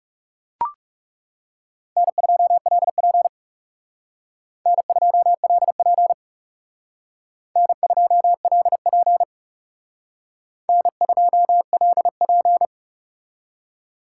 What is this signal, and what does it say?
Morse code, heard as a keyed tone, about 36 wpm, 700 Hz. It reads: N2LP N2LP N2LP N2LP
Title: Speed Racing - USA Call Signs 30wpm